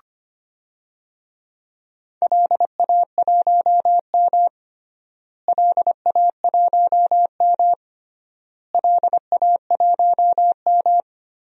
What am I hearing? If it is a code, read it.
Morse code, heard as a keyed tone, about 25 wpm, 700 Hz. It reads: LA1M LA1M LA1M
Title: DXCC-100 Call Signs - 4 Characters - Repeated 3x - 25wpm